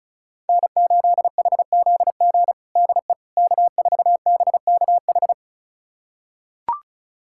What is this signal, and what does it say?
Morse code, heard as a keyed tone, about 36 wpm, 700 Hz. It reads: N8HZG DE K4BKH
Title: USA Callsign DE Callsign 35wpm